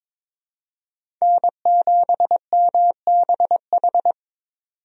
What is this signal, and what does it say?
Morse code, heard as a keyed tone, about 22 wpm, 700 Hz. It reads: N7MBH
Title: USA Call Signs - 5 Characters - 22wpm